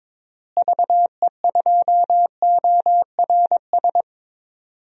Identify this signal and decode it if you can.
Morse code, heard as a keyed tone, about 22 wpm, 700 Hz. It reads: VE2ORS
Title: DXCC-100 Call Signs - 6 Characters - Repeated 3x - 22wpm